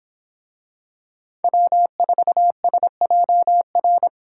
Morse code, heard as a keyed tone, about 24 wpm, 700 Hz. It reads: W4SJR